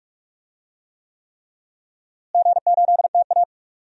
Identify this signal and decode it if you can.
Morse code, heard as a keyed tone, about 46 wpm, 700 Hz. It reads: G8TA
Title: DXCC-100 Call Signs - 4 Characters - Repeated 3x - 45wpm